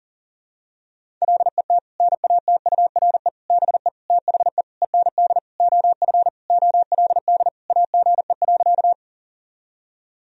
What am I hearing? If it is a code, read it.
Morse code, heard as a keyed tone, about 40 wpm, 700 Hz. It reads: LET NATURE BE THE END OF OLD AGE.
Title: Sentences from Top 1000 Words - Review Part 3 - 40wpm